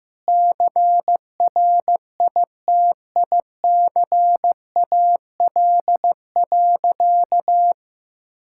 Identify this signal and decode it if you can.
Morse code, heard as a keyed tone, about 15 wpm, 700 Hz. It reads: CRITICAL.
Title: Sentences from Top 1600 Words 15wpm